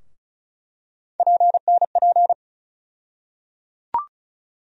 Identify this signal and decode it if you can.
Morse code, heard as a keyed tone, about 36 wpm, 700 Hz. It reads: PNP
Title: USA Call Sign Postfixes - 3 Letters 35wpm